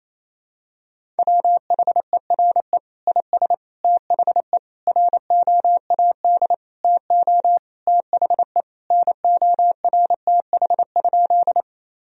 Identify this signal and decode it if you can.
Morse code, heard as a keyed tone, about 27 wpm, 700 Hz. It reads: WHERE IS THE ROAD TO THE NORTH?
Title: Sentences from Top 1000 Words - Review Part 2 - 28wpm